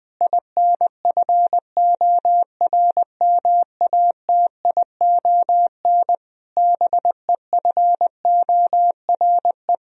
Morse code, heard as a keyed tone, about 20 wpm, 700 Hz. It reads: INFORMATION BEFORE
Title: Sentences from Top 1000 Words - Review Part 3 - 20wpm